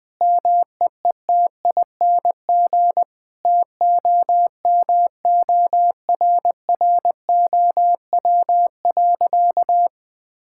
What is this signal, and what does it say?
Morse code, heard as a keyed tone, about 20 wpm, 700 Hz. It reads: MEETING TOMORROW.
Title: Sentences with 4-Word Phrases - 20wpm